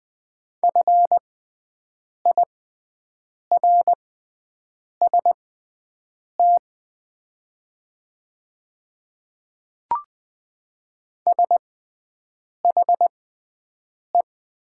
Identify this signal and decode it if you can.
Morse code, heard as a keyed tone, about 20 wpm, spaced as for 7 wpm, 700 Hz. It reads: FIRST SHE
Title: 6x Character Spacing - Top 100 Words - 20wpm